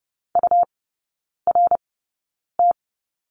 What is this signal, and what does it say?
Morse code, heard as a keyed tone, about 30 wpm, spaced as for 9 wpm, 700 Hz. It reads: URT
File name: Lesson 17e - Letter u - Three Characters 30wpm